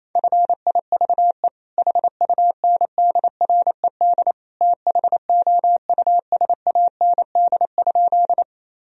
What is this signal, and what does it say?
Morse code, heard as a keyed tone, about 28 wpm, 700 Hz. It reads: FIVE HUNDRED THOUSAND?
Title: Sentences from Top 1000 Words - Review Part 3 - 28wpm